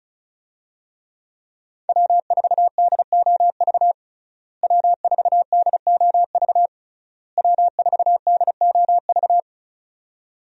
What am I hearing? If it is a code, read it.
Morse code, heard as a keyed tone, about 36 wpm, 700 Hz. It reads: W4DOV W4DOV W4DOV
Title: USA Call Signs - 5 Characters - Repeated 3x - 35wpm